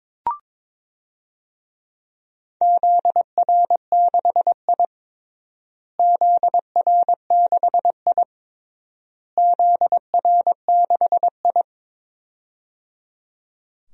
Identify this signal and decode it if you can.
Morse code, heard as a keyed tone, about 22 wpm, 700 Hz. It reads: ZR6I ZR6I ZR6I
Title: DXCC-100 Call Signs - 4 Characters - Repeated 3x - 22wpm